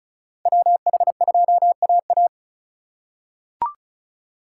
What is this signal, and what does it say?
Morse code, heard as a keyed tone, about 34 wpm, 690 Hz. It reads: WH2AA
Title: USA Call Signs - 5 Characters - 3x Character Spacing - 35wpm